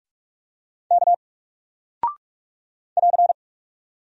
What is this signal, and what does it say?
Morse code, heard as a keyed tone, about 44 wpm, 700 Hz. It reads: K <AR>
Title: Single Character (Letter:Number:Punctuation:Prosign) Alt 45wpm